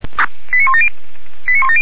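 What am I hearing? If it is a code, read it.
Five-tone selective call (ZVEI), heard as quick runs of five tones, tones of about 80 ms. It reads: 89170
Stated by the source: ZVEI five-tone selective call 89138 received by SDR, from the feuerwehr-orlamuende.org fire brigade server.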